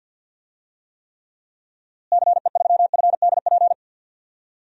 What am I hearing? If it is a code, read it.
Morse code, heard as a keyed tone, about 45 wpm, 700 Hz. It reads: CE3FDP
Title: DXCC-100 Call Signs - 6 Characters - Repeated 2x - 50wpm